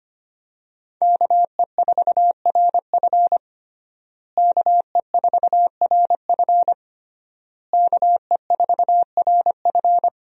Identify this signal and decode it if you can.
Morse code, heard as a keyed tone, about 25 wpm, 700 Hz. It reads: KE4RF KE4RF KE4RF
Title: USA Call Signs - 5 Characters - Repeated 3x - 25wpm